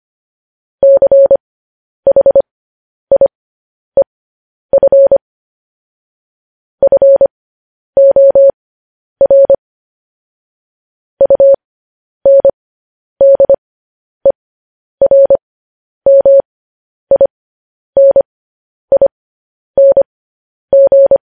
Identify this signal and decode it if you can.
Morse code, heard as a keyed tone, about 25 wpm, spaced as for 10 wpm, 550 Hz. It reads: CHIEF FOR UNDERMINING
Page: news headlines in morse code at 10 wpm